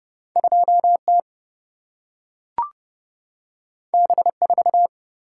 Morse code, heard as a keyed tone, about 30 wpm, 700 Hz. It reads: 2T B4
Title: Single-Letter Single-Number 30wpm